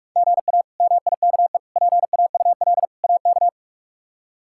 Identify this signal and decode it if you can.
Morse code, heard as a keyed tone, about 42 wpm, 700 Hz. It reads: GA MIKE PAUL AK
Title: K1USN SST QSO Practice - 45wpm